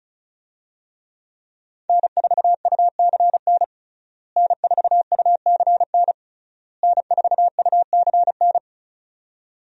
Morse code, heard as a keyed tone, about 34 wpm, 700 Hz. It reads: N4UCN N4UCN N4UCN
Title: USA Call Signs - 5 Characters - Repeated 3x - 35wpm